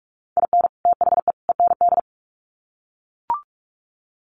Morse code, heard as a keyed tone, about 44 wpm, 700 Hz. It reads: IN THE END
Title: Sets of 3 Words 45wpm